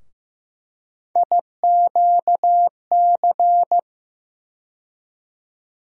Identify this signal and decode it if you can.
Morse code, heard as a keyed tone, about 15 wpm, 700 Hz. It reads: IQC
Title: USA Call Sign Postfixes - 3 Letters 15wpm